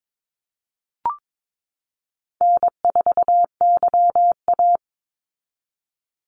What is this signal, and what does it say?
Morse code, heard as a keyed tone, about 22 wpm, 700 Hz. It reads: N4YA
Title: Easy Call Signs Repeated 3x - 22wpm